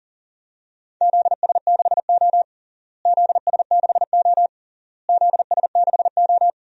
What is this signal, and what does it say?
Morse code, heard as a keyed tone, about 37 wpm, 700 Hz. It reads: ZS6O ZS6O ZS6O
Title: DXCC-100 Call Signs - 4 Characters - Repeated 3x - 40wpm